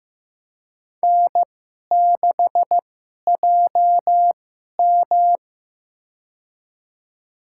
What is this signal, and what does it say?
Morse code, heard as a keyed tone, about 15 wpm, spaced as for 11 wpm, 700 Hz. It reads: N6JM
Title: USA Call Signs - 4 Characters - 2x Character Spacing - 15wpm